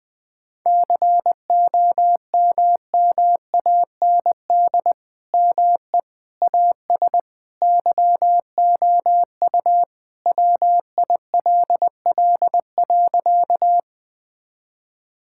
Morse code, heard as a keyed tone, about 20 wpm, 700 Hz. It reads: COMMAND ME AS YOU WILL.
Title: Sentences from Top 1600 Words 20wpm